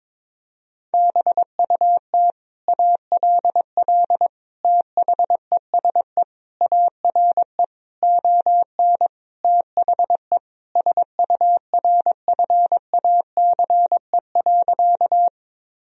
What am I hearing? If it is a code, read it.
Morse code, heard as a keyed tone, about 22 wpm, 700 Hz. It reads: BUT ALL THESE ARE ON THE SURFACE.
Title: Sentences from Top 1000 Words - Review Part 3 - 22wpm